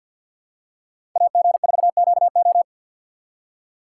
Morse code, heard as a keyed tone, about 45 wpm, 700 Hz. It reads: AG4QO
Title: USA Call Signs - 5 Characters - 2x Character Spacing - 50wpm